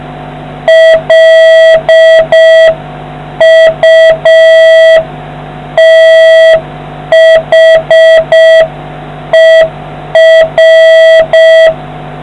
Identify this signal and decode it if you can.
Morse code, heard as a keyed tone, about 6 wpm, 650 Hz, with noise. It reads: LUTHER